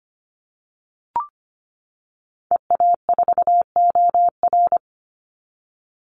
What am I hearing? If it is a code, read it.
Morse code, heard as a keyed tone, about 25 wpm, 700 Hz. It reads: EA4OR
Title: Call Signs - 25wpm